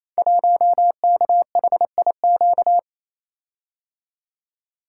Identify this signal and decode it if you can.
Morse code, heard as a keyed tone, about 29 wpm, 700 Hz. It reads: 1KHIQ
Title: Code Groups - 5 Letter:Numbers - 28wpm